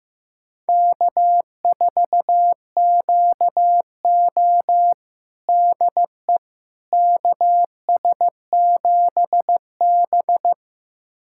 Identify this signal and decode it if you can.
Morse code, heard as a keyed tone, about 15 wpm, 700 Hz. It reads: K4QO DE KS7B
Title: USA Callsign DE Callsign - 4 Characters - 15wpm